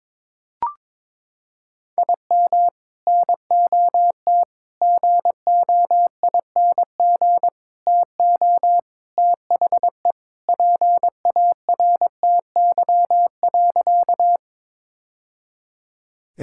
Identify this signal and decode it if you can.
Morse code, heard as a keyed tone, about 22 wpm, 700 Hz. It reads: IM NOT GOING TO THE PARTY.
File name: practice / 22wpm / Sentences with 3-Word Phrases - 22wpm